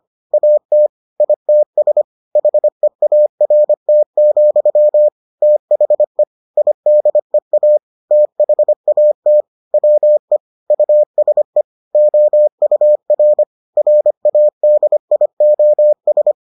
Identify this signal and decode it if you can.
Morse code, heard as a keyed tone, about 25 wpm, 600 Hz. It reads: AT ITS HEART, THE IDEA THAT WE USE OUR RADIOS